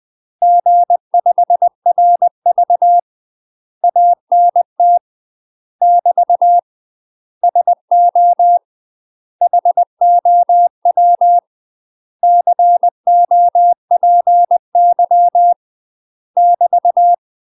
Morse code, heard as a keyed tone, about 20 wpm, 700 Hz. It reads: G5RV ANT <BT> SO HOW COPY <BT>
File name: UK MASTER QSOs TRAINING / 20WPM / UK MASTER QSOs 0030